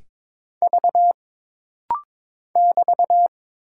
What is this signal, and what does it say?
Morse code, heard as a keyed tone, about 22 wpm, 700 Hz. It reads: V <BT>
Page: This rapid-fire timing is geared to push your ICR (Instant Character Recognition) skills.